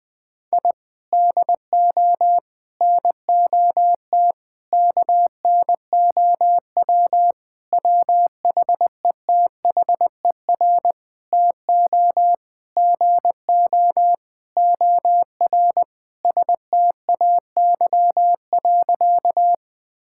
Morse code, heard as a keyed tone, about 21 wpm, 700 Hz. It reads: I DO NOT KNOW WHETHER TO GO OR STAY.
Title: Sentences with 5-Word Phrases - Spoken Beginning - 20wpm